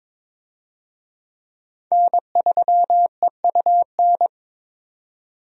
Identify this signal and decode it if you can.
Morse code, heard as a keyed tone, about 22 wpm, 700 Hz. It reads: N3EUN
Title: USA Call Signs - 5 Characters - 22wpm